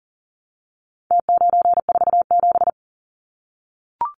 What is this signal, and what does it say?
Morse code, heard as a keyed tone, about 40 wpm, 700 Hz. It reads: T947
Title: Numbers - Contest Style - 4-Digit 40wpm